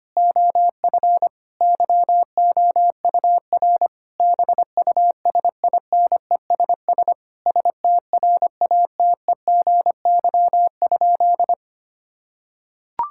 Morse code, heard as a keyed tone, about 25 wpm, 700 Hz. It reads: OF YOUR BUSINESS STRATEGY?
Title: Sentences from Top 1000 Words - Review Part 6 - 25wpm